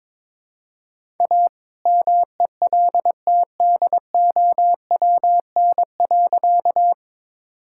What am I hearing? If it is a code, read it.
Morse code, heard as a keyed tone, about 22 wpm, 700 Hz. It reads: A MELTDOWN.
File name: All-Age Appropriate Jokes - 22wpm